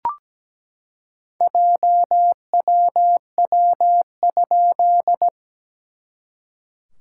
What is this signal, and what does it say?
Morse code, heard as a keyed tone, about 17 wpm, 700 Hz. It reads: JWW?
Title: SOTA QSO Practice 17wpm